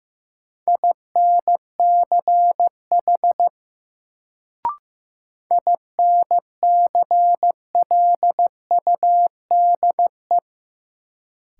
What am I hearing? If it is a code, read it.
Morse code, heard as a keyed tone, about 15 wpm, 700 Hz. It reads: INCH INCLUDE
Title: Sentences from Top 1000 Words - Review Part 3 - New Word Drill - 15wpm